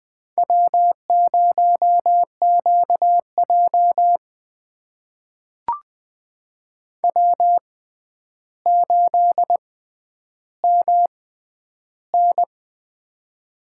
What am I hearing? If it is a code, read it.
Morse code, heard as a keyed tone, about 20 wpm, spaced as for 10 wpm, 700 Hz. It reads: W0QJ W8MN
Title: USA Call Signs - 4 Characters - 6x Character Spacing - 20wpm